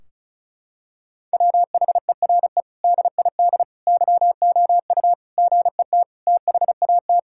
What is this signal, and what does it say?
Morse code, heard as a keyed tone, about 34 wpm, 700 Hz. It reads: WHERE DID YOU GET THAT
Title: Sentences from Top 1600 Words 35wpm